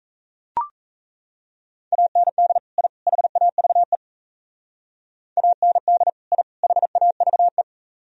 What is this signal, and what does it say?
Morse code, heard as a keyed tone, about 36 wpm, 700 Hz. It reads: AND I HAVE AND I HAVE